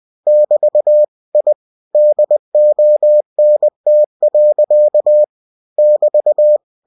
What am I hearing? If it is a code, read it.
Morse code, heard as a keyed tone, about 20 wpm, 600 Hz. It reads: <BT> I DONT. <BT>
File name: cw-books / winnie-the-pooh / winnie-20wpm-page-0000